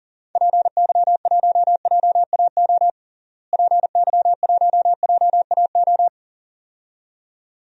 Morse code, heard as a keyed tone, about 40 wpm, 700 Hz. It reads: PY1JAO PY1JAO
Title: DXCC-100 Call Signs - 6 Characters - Repeated 2x - 40wpm